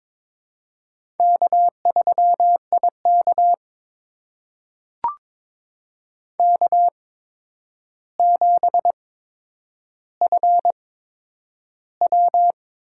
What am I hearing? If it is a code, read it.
Morse code, heard as a keyed tone, about 23 wpm, spaced as for 10 wpm, 700 Hz. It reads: K3IK K7FW
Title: USA Call Signs - 4 Characters - 8x Character Spacing - 22wpm